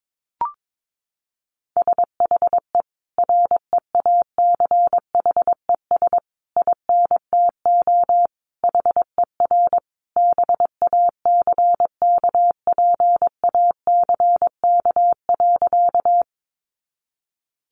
Morse code, heard as a keyed tone, about 22 wpm, 700 Hz. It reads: SHE REACHES INTO HER BACKPACK.
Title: Additional Easy Phrases 22wpm